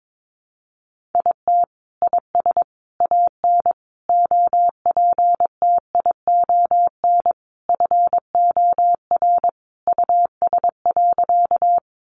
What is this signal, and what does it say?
Morse code, heard as a keyed tone, about 22 wpm, 700 Hz. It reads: IT IS AN OPTION FOR US.
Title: Sentences from Top 1300 Words 22wpm